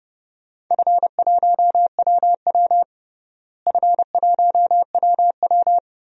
Morse code, heard as a keyed tone, about 30 wpm, 700 Hz. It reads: F1WW F1WW
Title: DXCC-100 Call Signs - 4 Characters - Repeated 3x - 30wpm